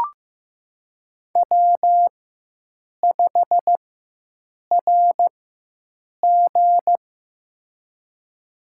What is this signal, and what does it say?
Morse code, heard as a keyed tone, about 15 wpm, spaced as for 7 wpm, 700 Hz. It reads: W5RG
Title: USA Call Signs - 4x Character Spacing - 15wpm